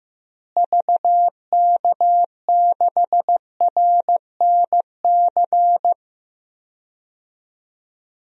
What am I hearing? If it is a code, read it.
Morse code, heard as a keyed tone, about 15 wpm, 700 Hz. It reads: VK6RNC
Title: DXCC-100 Call Signs - 6 Characters - Repeated 3x - 15wpm